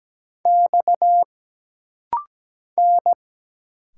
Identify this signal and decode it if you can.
Morse code, heard as a keyed tone, about 17 wpm, 700 Hz. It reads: X N
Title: Single Letters - Rapid-Fire 17wpm